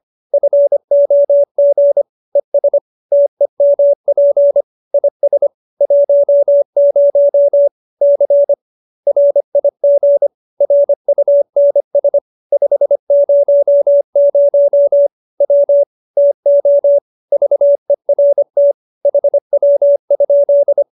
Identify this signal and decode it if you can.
Morse code, heard as a keyed tone, about 25 wpm, 570 Hz. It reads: FOG ES TEMP IS 10 C RIG RUNS 500 W TO VERT HW?